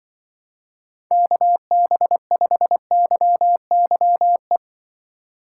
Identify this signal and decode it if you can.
Morse code, heard as a keyed tone, about 24 wpm, 700 Hz. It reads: KB5YYE
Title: Speed Racing - USA Call Signs 20wpm